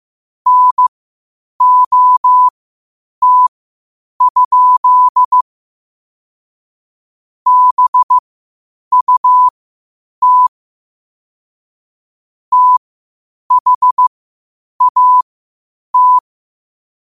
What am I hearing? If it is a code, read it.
Morse code, heard as a keyed tone, about 15 wpm, spaced as for 8 wpm, 1000 Hz. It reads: NOT? BUT THAT
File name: qotdfiles / 8wpm